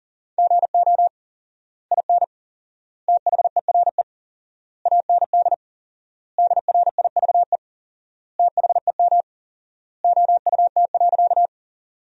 Morse code, heard as a keyed tone, about 40 wpm, 700 Hz. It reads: GO IN THERE AND DRIVE THEM OUT.
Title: 4x Word-Spacing - Sentences from Top 500 Words - 40wpm